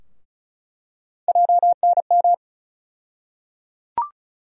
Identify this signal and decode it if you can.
Morse code, heard as a keyed tone, about 34 wpm, 700 Hz. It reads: JNM
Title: USA Call Sign Postfixes - 3 Letters 35wpm